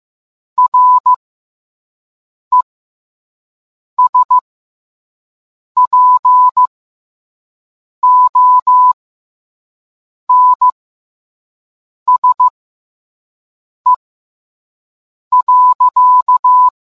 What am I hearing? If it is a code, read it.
Morse code, heard as a keyed tone, about 15 wpm, spaced as for 5 wpm, 1000 Hz. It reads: RESPONSE.